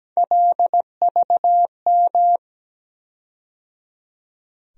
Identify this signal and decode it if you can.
Morse code, heard as a keyed tone, about 17 wpm, 700 Hz. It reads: LVM
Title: USA Call Sign Postfixes - 3 Letters 17wpm